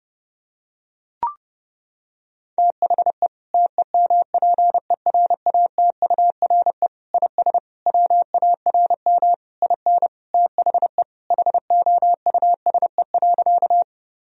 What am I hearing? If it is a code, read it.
Morse code, heard as a keyed tone, about 30 wpm, 700 Hz. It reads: THE TEMPERATURE IS WARM IN THE HOUSE.
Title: Sentences from Top 1000 Words - Review Part 4 - 30wpm